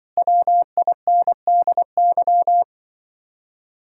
Morse code, heard as a keyed tone, about 24 wpm, 700 Hz. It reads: WINDY
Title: Speed-Racing Common QSO Phrases - 20wpm